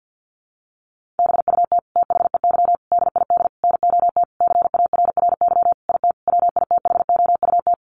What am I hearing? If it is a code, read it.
Morse code, heard as a keyed tone, about 50 wpm, 700 Hz. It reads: BUT THEY DID NOT CARRY IT WITHOUT
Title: Sentences from Top 1400 Words 50wpm